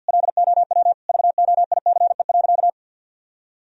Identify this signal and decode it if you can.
Morse code, heard as a keyed tone, about 47 wpm, 700 Hz. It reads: LOW VOICE.
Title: Sentences from Top 1000 Words - Review Part 1 - 50wpm